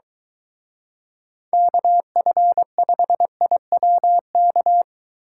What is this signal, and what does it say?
Morse code, heard as a keyed tone, about 23 wpm, 700 Hz. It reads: KF5IWK